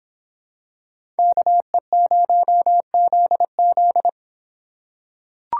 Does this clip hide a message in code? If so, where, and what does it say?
Yes: Morse code from 1.2 to 4.1 s, heard as a keyed tone, about 23 wpm, 700 Hz. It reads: KE0ZZ